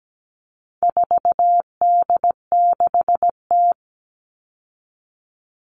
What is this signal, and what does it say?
Morse code, heard as a keyed tone, about 17 wpm, 700 Hz. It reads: 4D6T
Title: All Cut Numbers - 4-Digit 17wpm